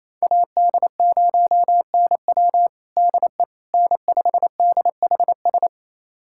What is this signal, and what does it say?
Morse code, heard as a keyed tone, about 28 wpm, 700 Hz. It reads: AD0NW DE N5DHS
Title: USA Callsign DE Callsign - 5 Characters - 28wpm